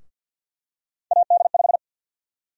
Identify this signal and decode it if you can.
Morse code, heard as a keyed tone, about 53 wpm, 700 Hz. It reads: AD5
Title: USA Call Sign Prefixes - 50wpm